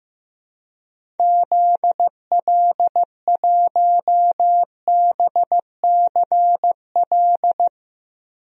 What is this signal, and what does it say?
Morse code, heard as a keyed tone, about 15 wpm, 700 Hz. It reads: ZL1BCL
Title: DXCC-100 Call Signs - 6 Characters - 15wpm